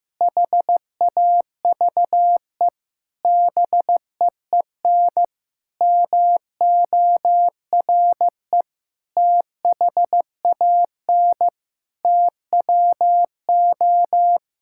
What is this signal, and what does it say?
Morse code, heard as a keyed tone, about 15 wpm, 700 Hz. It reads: HAVE BEEN MORE THAN TWO
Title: Sentences from Top 1600 Words 15wpm